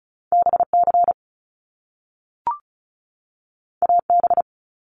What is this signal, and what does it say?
Morse code, heard as a keyed tone, about 36 wpm, 700 Hz. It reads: BC AB
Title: Canadian Provinces and Territories 35wpm